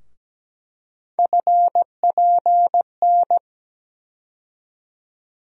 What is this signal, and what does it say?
Morse code, heard as a keyed tone, about 17 wpm, 700 Hz. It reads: FPN
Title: USA Call Sign Postfixes - 3 Letters 17wpm